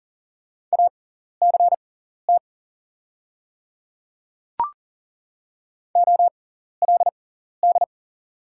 Morse code, heard as a keyed tone, about 40 wpm, spaced as for 14 wpm, 690 Hz. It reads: ACT OLD
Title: 6x Character Spacing - Top 200 Words - 40wpm